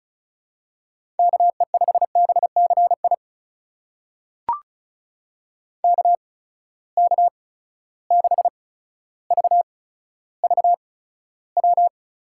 Morse code, heard as a keyed tone, about 34 wpm, spaced as for 15 wpm, 700 Hz. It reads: KE5BCI KK6VVW
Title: USA Call Signs - 6 Characters - 8x Character Spacing - 35wpm